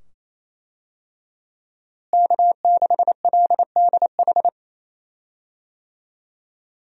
Morse code, heard as a keyed tone, about 27 wpm, 700 Hz. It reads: K6LDH
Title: USA Call Signs - 5 Characters - Repeated 2x - 28wpm